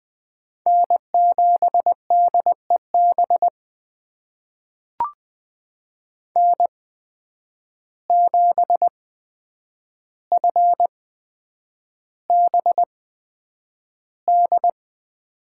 Morse code, heard as a keyed tone, about 20 wpm, spaced as for 9 wpm, 700 Hz. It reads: N7DEB N7FBD
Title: USA Call Signs - 5 Characters - 8x Character Spacing - 20wpm